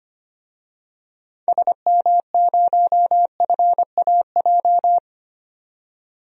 Morse code, heard as a keyed tone, about 25 wpm, 700 Hz. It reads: SM0FAJ
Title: DXCC-100 Call Signs - 6 Characters - 25wpm